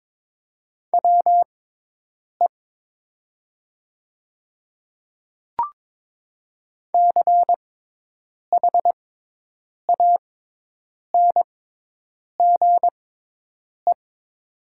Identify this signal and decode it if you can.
Morse code, heard as a keyed tone, about 22 wpm, spaced as for 8 wpm, 700 Hz. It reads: WE CHANGE
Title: 6x Character Spacing - Top 200 Words - 22wpm